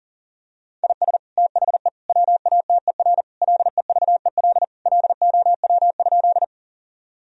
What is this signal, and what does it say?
Morse code, heard as a keyed tone, about 40 wpm, 700 Hz. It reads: IS THE WATER LEVEL LOW?
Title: Sentences from Top 1000 Words - Review Part 4 - 40wpm